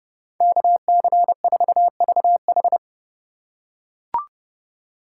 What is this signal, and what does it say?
Morse code, heard as a keyed tone, about 30 wpm, 700 Hz. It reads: KC4VH